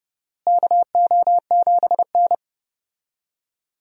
Morse code, heard as a keyed tone, about 30 wpm, 700 Hz. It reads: KO7N
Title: USA Call Signs - 4 Characters - Repeated 3x - 30wpm